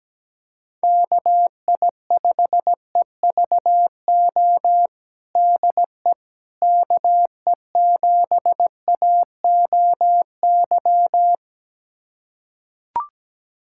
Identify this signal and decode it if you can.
Morse code, heard as a keyed tone, about 17 wpm, 700 Hz. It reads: KI5EVO DE KE7AOY
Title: USA Callsign DE Callsign 17wpm